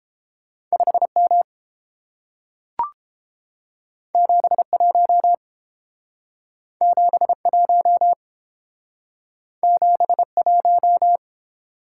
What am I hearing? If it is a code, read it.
Morse code, heard as a keyed tone, about 30 wpm, 700 Hz. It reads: 5M 71 71 71